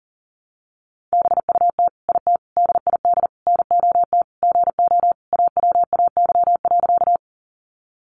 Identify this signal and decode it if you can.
Morse code, heard as a keyed tone, about 40 wpm, 700 Hz. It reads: BUT IT DID NOT GO AWAY.
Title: Sentences from Top 600 Words 40wpm